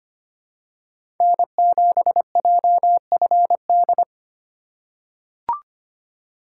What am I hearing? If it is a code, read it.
Morse code, heard as a keyed tone, about 25 wpm, 700 Hz. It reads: N7JFD
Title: USA Call Signs - 5 Characters - 2x Character Spacing - 25wpm